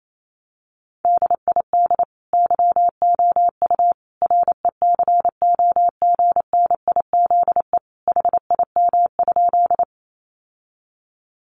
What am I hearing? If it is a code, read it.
Morse code, heard as a keyed tone, about 28 wpm, 700 Hz. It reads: DID YOU RECOGNIZE HIM?
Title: Sentences from Top 1400 Words 28wpm